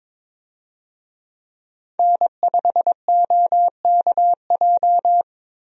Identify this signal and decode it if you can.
Morse code, heard as a keyed tone, about 22 wpm, 700 Hz. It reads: N5OKJ
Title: USA Call Signs - 5 Characters - Repeated 2x - 22wpm